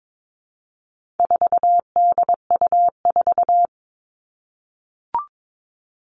Morse code, heard as a keyed tone, about 22 wpm, 700 Hz. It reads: 4DU4
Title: All Cut Numbers - 4-Digit 22wpm